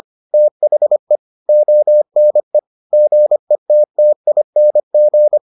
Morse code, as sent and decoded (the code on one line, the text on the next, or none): - .... / --- -. / --. - - .. -. --.
THE ONE GETTING